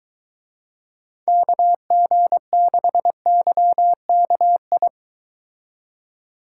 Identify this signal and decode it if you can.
Morse code, heard as a keyed tone, about 23 wpm, 700 Hz. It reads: KG6YKI